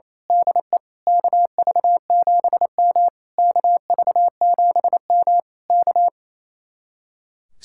Morse code, heard as a keyed tone, about 27 wpm, 700 Hz. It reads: DE KV7M KV7M K
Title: Speed-Racing Common QSO Phrases - 28wpm